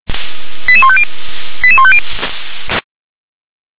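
Five-tone selective call (ZVEI), heard as quick runs of five tones, tones of about 70 ms. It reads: 88150 88150
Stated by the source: ZVEI five-tone selective call received on an SDR.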